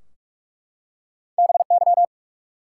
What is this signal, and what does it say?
Morse code, heard as a keyed tone, about 44 wpm, 700 Hz. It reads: BY
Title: Two Letter Words - 45wpm